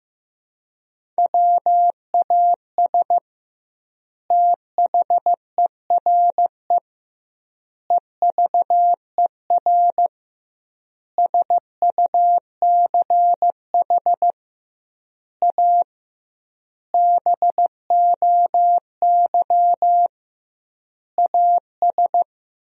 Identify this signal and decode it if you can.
Morse code, heard as a keyed tone, about 15 wpm, 700 Hz. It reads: WAS THERE EVER SUCH A BOY AS
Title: 2x Word-Spacing - Sentences from Top 400 Words - 15wpm